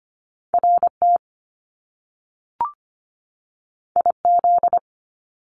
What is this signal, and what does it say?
Morse code, heard as a keyed tone, about 25 wpm, 700 Hz. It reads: RT IZ
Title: Top Two Letter Combinations in Words 25wpm